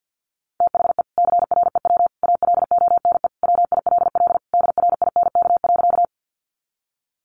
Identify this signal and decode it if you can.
Morse code, heard as a keyed tone, about 50 wpm, 700 Hz. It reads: THE CREW ALONE WILL DRINK.
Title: Sentences from Top 1500 Words 50wpm